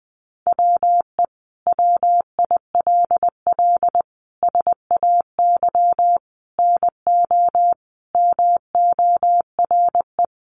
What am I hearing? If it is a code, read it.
Morse code, heard as a keyed tone, about 20 wpm, 700 Hz. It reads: WE WILL SAY NO MORE
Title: Sentences from Top 200 Words 20wpm